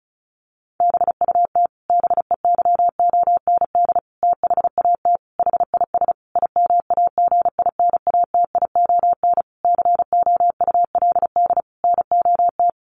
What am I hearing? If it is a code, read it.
Morse code, heard as a keyed tone, about 34 wpm, 700 Hz. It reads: BUT BEYOND THAT HIS IMAGINATION COULD NOT
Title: Sentences from Top 1300 Words 35wpm